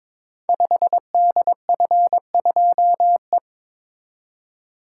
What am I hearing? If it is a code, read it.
Morse code, heard as a keyed tone, about 22 wpm, 700 Hz. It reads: 5DF2E
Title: Code Groups - 5 Letter:Numbers - 22wpm